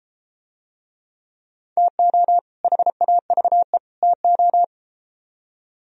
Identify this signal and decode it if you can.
Morse code, heard as a keyed tone, about 32 wpm, 700 Hz. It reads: TO HAVE TO